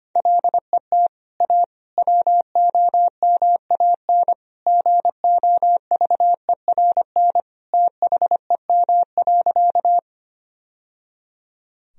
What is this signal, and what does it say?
Morse code, heard as a keyed tone, about 25 wpm, 700 Hz. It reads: LET A WOMAN GOVERN THEM.
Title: Sentences from Top 1000 Words - Review Part 2 - 25wpm